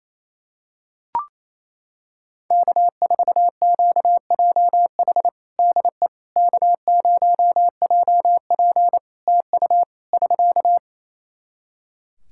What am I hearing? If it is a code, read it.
Morse code, heard as a keyed tone, about 28 wpm, 700 Hz. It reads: K4QJH DE K0JP TU <SK>
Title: Practice QSO - Exchange 4 - Ending - 28wpm